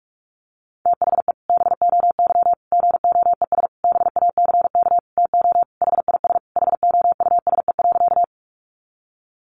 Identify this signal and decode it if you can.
Morse code, heard as a keyed tone, about 44 wpm, 700 Hz. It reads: THE BOY GOES BACK TO HIS HOUSE.
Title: Easy Phrases 45wpm